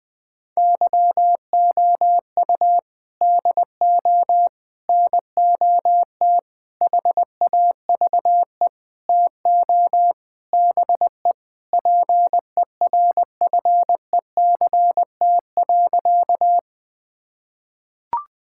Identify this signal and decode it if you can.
Morse code, heard as a keyed tone, about 20 wpm, 700 Hz. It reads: YOU DO NOT HAVE TO BE PERFECT.
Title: Sentences with 5-Word Phrases - 20wpm